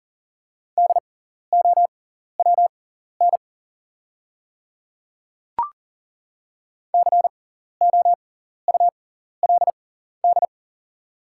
Morse code, heard as a keyed tone, about 40 wpm, spaced as for 14 wpm, 700 Hz. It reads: DOWN COULD
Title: 6x Character Spacing - Top 100 Words - 40wpm